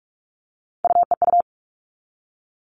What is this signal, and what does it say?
Morse code, heard as a keyed tone, about 44 wpm, 700 Hz. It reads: UEU
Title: All Cut Numbers - 3-Digit 45wpm